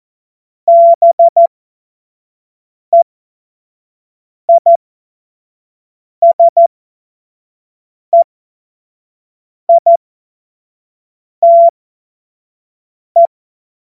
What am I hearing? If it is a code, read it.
Morse code, heard as a keyed tone, about 14 wpm, spaced as for 5 wpm, 680 Hz. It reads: BEISEITE
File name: Morsekurs / Witze / Witz 01. Juni 2021, 25 F70